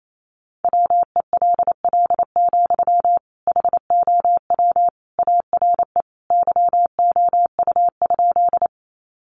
Morse code, heard as a keyed tone, about 28 wpm, 700 Hz. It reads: WELL, HOW ARE YOU?
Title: Sentences from Top 200 Words 28wpm